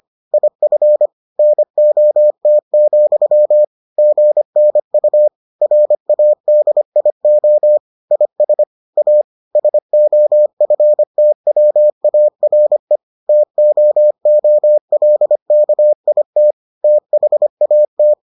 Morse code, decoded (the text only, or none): IF NOT, GNU RADIO IS A SOFTWARE TOOLKIT THAT